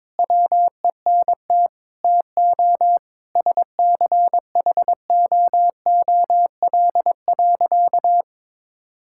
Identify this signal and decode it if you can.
Morse code, heard as a keyed tone, about 22 wpm, 700 Hz. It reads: WENT TO SCHOOL.
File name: practice / 22wpm / Sentences with 4-Word Phrases - 22wpm